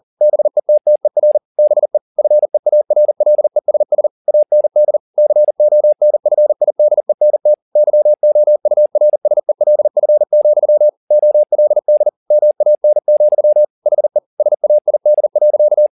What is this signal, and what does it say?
Morse code, heard as a keyed tone, about 40 wpm, 600 Hz. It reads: BETTER BE FEARLESS AND CONFIDENT YOURSELF, OLD MAN, HE SAID.
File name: the old man and the sea 2 40wpm